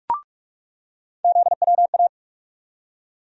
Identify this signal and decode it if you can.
Morse code, heard as a keyed tone, about 47 wpm, 700 Hz. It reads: ZWA
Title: USA Call Sign Postfixes - 3 Letters 45wpm